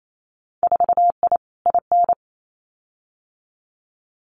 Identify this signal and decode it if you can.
Morse code, heard as a keyed tone, about 28 wpm, 700 Hz. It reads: 4I IN
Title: Winter Field Day Exchanges 28wpm